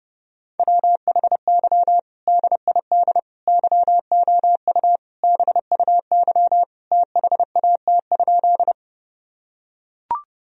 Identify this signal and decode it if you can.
Morse code, heard as a keyed tone, about 30 wpm, 700 Hz. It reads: WHY DID YOU BUY THAT?
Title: Sentences from Top 1000 Words - Review Part 4 - 30wpm